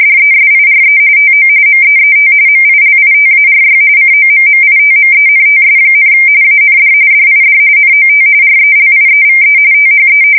RTTY, about 45.45 baud. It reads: DESIRE TO MEET MILITARY COMMITMENTS, AND DID NOT RELIEVE THE
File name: fsk170 / USS-New-Jersey